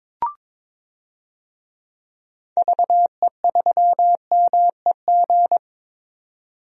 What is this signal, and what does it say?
Morse code, heard as a keyed tone, about 22 wpm, 700 Hz. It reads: VE3MEG
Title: DXCC-100 Call Signs - 6 Characters - 22wpm